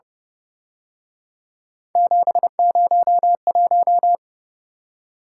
Morse code, heard as a keyed tone, about 30 wpm, 700 Hz. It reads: 701
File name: Speed-Racing - Three Letter-Number Combinations - 25wpm